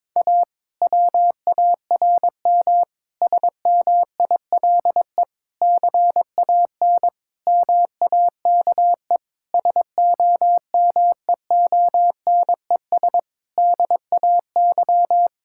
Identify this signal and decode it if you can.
Morse code, heard as a keyed tone, about 22 wpm, 700 Hz. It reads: A WARM SMILE CAN MAKE SOMEONES DAY
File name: Difficult Sentences with Hints - 22wpm